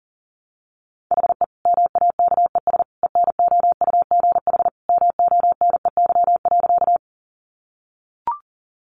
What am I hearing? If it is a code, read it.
Morse code, heard as a keyed tone, about 37 wpm, 700 Hz. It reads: HE MAKES ENOUGH MONEY.
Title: Additional Easy Phrases 40wpm